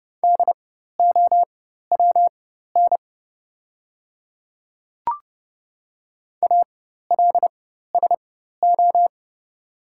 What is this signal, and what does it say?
Morse code, heard as a keyed tone, about 30 wpm, spaced as for 14 wpm, 690 Hz. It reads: DOWN ALSO
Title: 4x Character Spacing - Top 200 Words - 30wpm